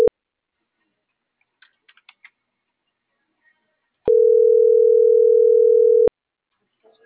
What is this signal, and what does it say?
Ringback tone (North America), 440 + 480 Hz.